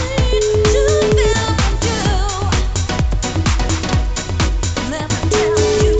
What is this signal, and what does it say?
Ringback tone (Europe, 425 Hz), with noise.